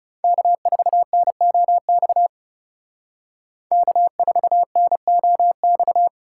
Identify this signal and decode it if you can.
Morse code, heard as a keyed tone, about 38 wpm, 700 Hz. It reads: K4NOX K4NOX